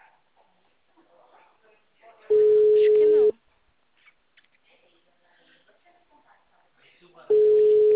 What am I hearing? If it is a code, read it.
Ringback tone (Europe, 425 Hz), with speech.
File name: out- -2001- - - .45899